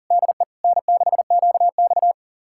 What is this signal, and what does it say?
Morse code, heard as a keyed tone, about 48 wpm, 700 Hz. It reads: DE N6QX
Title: Speed-Racing Common QSO Phrases Encore - 40wpm